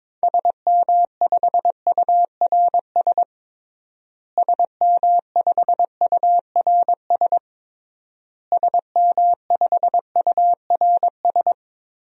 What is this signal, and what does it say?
Morse code, heard as a keyed tone, about 22 wpm, 700 Hz. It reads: SM5URS SM5URS SM5URS
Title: DXCC-100 Call Signs - 6 Characters - Repeated 3x - 22wpm